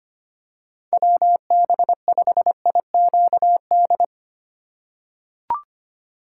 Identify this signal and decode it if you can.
Morse code, heard as a keyed tone, about 25 wpm, 700 Hz. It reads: WB5IQD
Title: USA Call Signs - 6 Characters - 3x Character Spacing - 25wpm